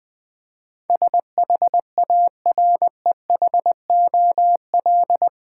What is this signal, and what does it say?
Morse code, heard as a keyed tone, about 20 wpm, 700 Hz. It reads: SHAREHOL
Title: Build A Word - 4 Letters At A Time - 11-Letter Words - 20wpm